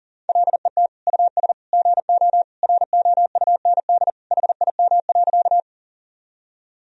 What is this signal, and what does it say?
Morse code, heard as a keyed tone, about 37 wpm, 700 Hz. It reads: LET US GO ROUND HIM.
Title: Sentences from Top 1000 Words - Review Part 1 - 40wpm